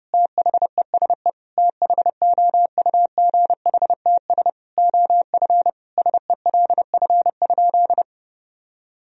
Morse code, heard as a keyed tone, about 30 wpm, 700 Hz. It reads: THESE THOUGHTS OF SELF?
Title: Sentences from Top 1000 Words - Review Part 1 - 30wpm